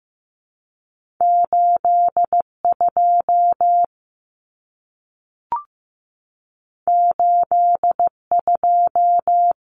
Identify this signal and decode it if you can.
Morse code, heard as a keyed tone, about 15 wpm, 700 Hz. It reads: 82 82
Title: Common Cut Numbers - 2-Digit 15wpm